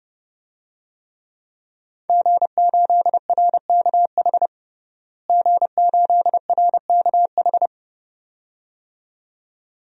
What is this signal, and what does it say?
Morse code, heard as a keyed tone, about 30 wpm, 700 Hz. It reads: G8RKH G8RKH
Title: DXCC-100 Call Signs - 5 Characters - Repeated 2x - 30wpm